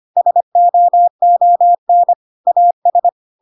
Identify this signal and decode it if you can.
Morse code, heard as a keyed tone, about 25 wpm, 700 Hz. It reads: SOON AS
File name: Pinocchio / Pinocchio ch1 25 25 ik1hge